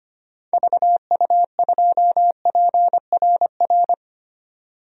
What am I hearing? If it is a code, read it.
Morse code, heard as a keyed tone, about 25 wpm, 700 Hz. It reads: VU2PRR
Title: DXCC-100 Call Signs - 6 Characters - 25wpm